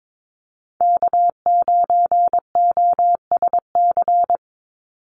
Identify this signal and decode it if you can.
Morse code, heard as a keyed tone, about 22 wpm, 700 Hz. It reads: K9OSC
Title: NA SOTA Chaser Callsigns 22wpm